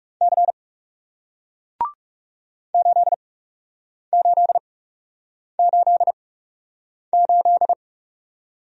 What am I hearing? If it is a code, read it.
Morse code, heard as a keyed tone, about 41 wpm, spaced as for 8 wpm, 700 Hz. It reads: C 8888